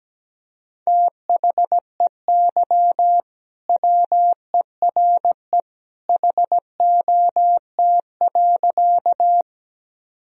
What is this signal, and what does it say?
Morse code, heard as a keyed tone, about 17 wpm, 700 Hz. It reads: THEY WERE HOT.
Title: Sentences from Top 1000 Words - Review Part 3 - 17wpm